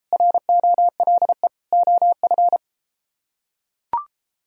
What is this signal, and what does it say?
Morse code, heard as a keyed tone, about 31 wpm, 700 Hz. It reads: ROLE OF